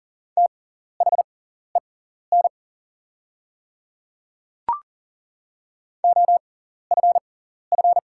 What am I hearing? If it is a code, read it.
Morse code, heard as a keyed tone, about 40 wpm, spaced as for 14 wpm, 690 Hz. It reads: THEN OFF
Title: 6x Character Spacing - Top 200 Words - 40wpm